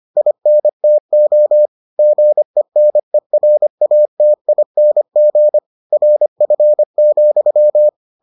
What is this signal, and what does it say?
Morse code, heard as a keyed tone, about 25 wpm, 600 Hz. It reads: INTO GENERATING RF,